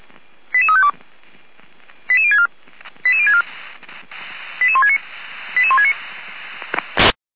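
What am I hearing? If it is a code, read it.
Five-tone selective call (ZVEI), heard as quick runs of five tones, tones of about 80 ms. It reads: 88352 88064 88064 88169 88169
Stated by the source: ZVEI five-tone selective call 88063 received on SDR.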